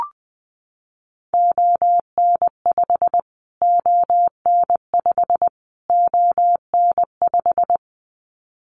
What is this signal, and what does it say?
Morse code, heard as a keyed tone, about 20 wpm, 700 Hz. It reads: ON5 ON5 ON5
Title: Callsign Prefixes for Top 25 DXCC - Repeated 3x - 20wpm